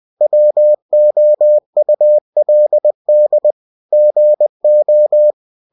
Morse code, heard as a keyed tone, about 20 wpm, 600 Hz. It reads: WOULD GO
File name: cw-books / winnie-the-pooh / winnie-20wpm-page-0041